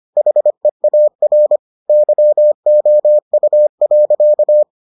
Morse code, heard as a keyed tone, about 25 wpm, 600 Hz. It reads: HEAR YOU.